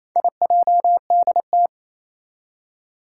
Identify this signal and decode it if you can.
Morse code, heard as a keyed tone, about 27 wpm, 700 Hz. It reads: IJDT
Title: Four Challenging Letter-Number Combinations 28wpm